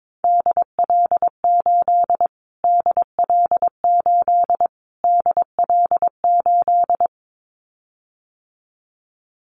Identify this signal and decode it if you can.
Morse code, heard as a keyed tone, about 22 wpm, 700 Hz. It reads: DL8 DL8 DL8
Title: Callsign Prefixes for Top 25 DXCC - Repeated 3x - 22wpm